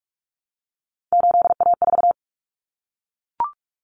Morse code, heard as a keyed tone, about 44 wpm, 700 Hz. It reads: 8A4
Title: Common Cut Numbers - 3-Digit 45wpm